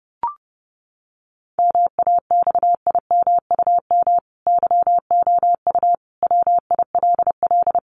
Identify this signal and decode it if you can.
Morse code, heard as a keyed tone, about 30 wpm, 700 Hz. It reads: MAXIMUM YOU WILL
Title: Sentences from Top 1300 Words 30wpm